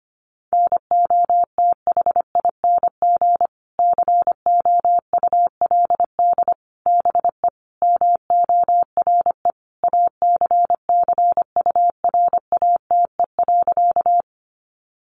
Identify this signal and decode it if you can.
Morse code, heard as a keyed tone, about 25 wpm, 700 Hz. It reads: NOTHING COULD BE MORE ACCURATE.
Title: Sentences from Top 1200 Words 25wpm